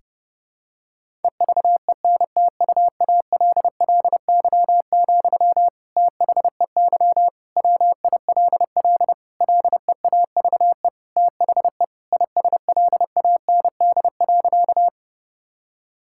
Morse code, heard as a keyed tone, about 32 wpm, 700 Hz. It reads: EVENTUALLY, THEY WILL LEAVE THE ISLAND.
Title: Sentences from Top 1000 Words - Review Part 6 - 30wpm